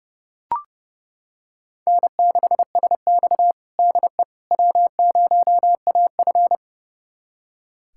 Morse code, heard as a keyed tone, about 31 wpm, 700 Hz. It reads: N6SX DE W0AF
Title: USA Callsign DE Callsign - 4 Characters - 30wpm